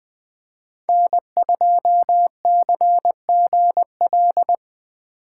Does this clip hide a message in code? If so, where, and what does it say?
Yes: Morse code from 0.9 to 4.5 s, heard as a keyed tone, about 21 wpm, 700 Hz. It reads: N2CGL